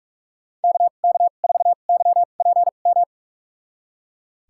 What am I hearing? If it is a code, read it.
Morse code, heard as a keyed tone, about 41 wpm, spaced as for 31 wpm, 700 Hz. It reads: KK4YPM
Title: USA Call Signs - 6 Characters - 2x Character Spacing - 45wpm